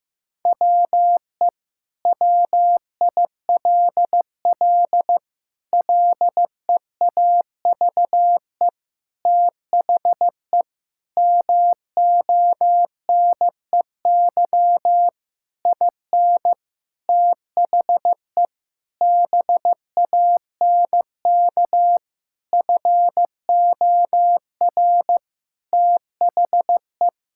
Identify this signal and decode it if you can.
Morse code, heard as a keyed tone, about 15 wpm, 700 Hz. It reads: WE WILL LEAVE THE MONEY IN THE BANK FOR THE
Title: Sentences from Top 1000 Words - Review Part 6 - 15wpm